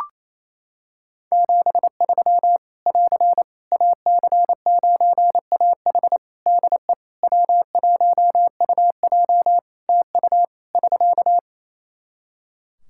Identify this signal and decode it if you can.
Morse code, heard as a keyed tone, about 28 wpm, 700 Hz. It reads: 73 <AR> AC9AH DE W1UJ TU <SK>
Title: K1USN SST QSO Practice 28wpm